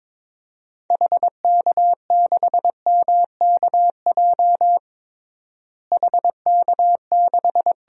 Morse code, heard as a keyed tone, about 22 wpm, 700 Hz. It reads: HK6MKJ HK6
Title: DXCC-100 Call Signs - 6 Characters - Repeated 3x - 22wpm